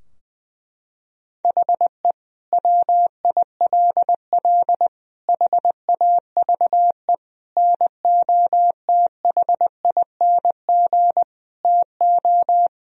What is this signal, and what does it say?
Morse code, heard as a keyed tone, about 21 wpm, 700 Hz. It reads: HE WILL HAVE NOTHING TO
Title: Sentences from Top 1000 Words - Review Part 4 - 20wpm